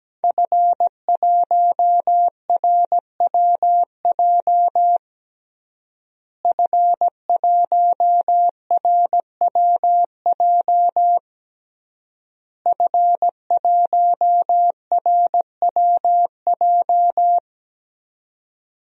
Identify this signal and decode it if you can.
Morse code, heard as a keyed tone, about 17 wpm, 700 Hz. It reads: F1RWJ F1RWJ F1RWJ
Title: DXCC-100 Call Signs - 5 Characters - Repeated 3x - 17wpm